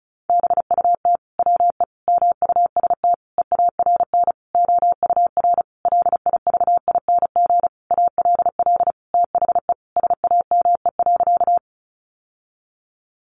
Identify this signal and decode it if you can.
Morse code, heard as a keyed tone, about 34 wpm, 700 Hz. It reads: BUT WE MUST EARN OUR LIVING ALL THE SAME.
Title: Sentences from Top 1500 Words 35wpm